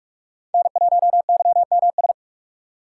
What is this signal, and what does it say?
Morse code, heard as a keyed tone, about 46 wpm, 700 Hz. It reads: N1YMS
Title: USA Call Signs - 5 Characters - 2x Character Spacing - 45wpm